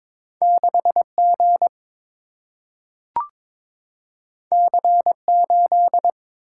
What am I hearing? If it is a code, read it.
Morse code, heard as a keyed tone, about 22 wpm, 700 Hz. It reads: 6G C8
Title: Single-Letter Single-Number 22wpm